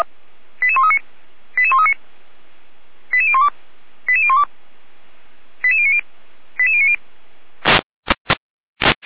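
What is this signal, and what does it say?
Five-tone selective call (ZVEI), heard as quick runs of five tones, tones of about 70 ms. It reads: 88139 88139 88012 88012 88090 88090
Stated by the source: ZVEI five-tone call received by SDR, from the Feuerwehr Orlamünde (fire brigade) server audio.f85360.feuerwehr-orlamuende.org.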